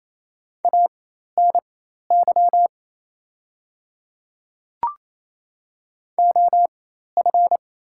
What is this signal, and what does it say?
Morse code, heard as a keyed tone, about 27 wpm, spaced as for 13 wpm, 690 Hz. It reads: ANY OF
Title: 4x Character Spacing - Top 200 Words - 28wpm